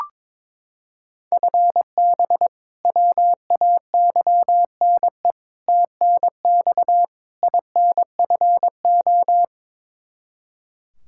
Morse code, heard as a keyed tone, about 22 wpm, 700 Hz. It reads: FB WAYNE TNX INFO
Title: Practice QSO - Exchange 3 - More Info - 22wpm